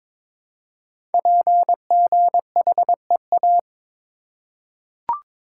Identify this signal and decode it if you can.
Morse code, heard as a keyed tone, about 22 wpm, 700 Hz. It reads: PGHEA
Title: Code Groups - 5 Letters - 22wpm